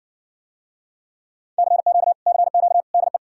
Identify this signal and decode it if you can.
Morse code, heard as a keyed tone, about 60 wpm, 700 Hz. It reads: CQ CQ DE